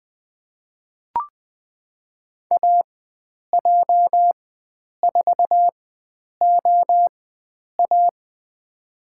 Morse code, heard as a keyed tone, about 20 wpm, spaced as for 9 wpm, 700 Hz. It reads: AJ4OA
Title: USA Call Signs - 4x Character Spacing - 20wpm